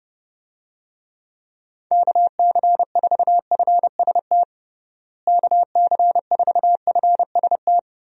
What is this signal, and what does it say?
Morse code, heard as a keyed tone, about 29 wpm, 700 Hz. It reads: KC4FST KC4FST
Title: USA Call Signs - 6 Characters - Repeated 2x - 30wpm